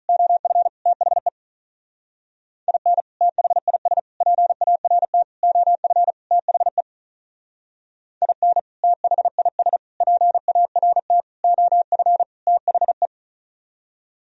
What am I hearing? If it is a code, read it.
Morse code, heard as a keyed tone, about 44 wpm, 700 Hz. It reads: OF THE IN THIS PART OF THE IN THIS PART OF THE